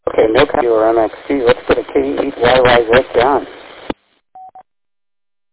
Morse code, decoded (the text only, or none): none